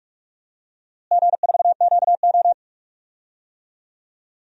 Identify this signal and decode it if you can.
Morse code, heard as a keyed tone, about 43 wpm, 700 Hz. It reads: G4QO
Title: DXCC-100 Call Signs - 4 Characters - Repeated 3x - 45wpm